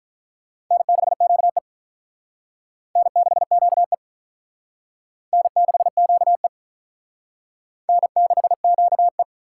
Morse code, heard as a keyed tone, about 44 wpm, 700 Hz. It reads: N6QE N6QE N6QE N6QE
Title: Speed Racing - USA Call Signs 35wpm